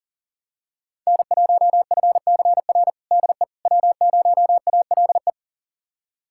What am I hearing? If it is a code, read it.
Morse code, heard as a keyed tone, about 40 wpm, 700 Hz. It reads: N1FCR DE W0ALE
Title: USA Callsign DE Callsign - 5 Characters - 40wpm